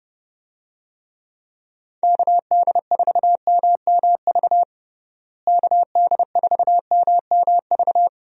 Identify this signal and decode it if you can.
Morse code, heard as a keyed tone, about 30 wpm, 700 Hz. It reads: KD4MMV KD4MMV
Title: USA Call Signs - 6 Characters - Repeated 3x - 30wpm